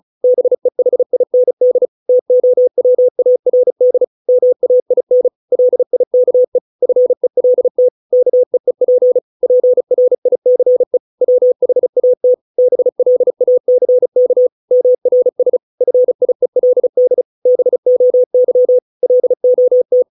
Morse code, heard as a keyed tone, about 34 wpm, 500 Hz. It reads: BEHIND TOWARD MAIN LIKE FELT KEEP PRICE WHAT BLACK MRS FIELD BOY LOT